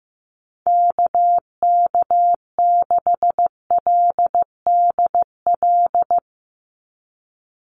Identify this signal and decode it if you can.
Morse code, heard as a keyed tone, about 15 wpm, 700 Hz. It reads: KK6LDL
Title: USA Call Signs 15wpm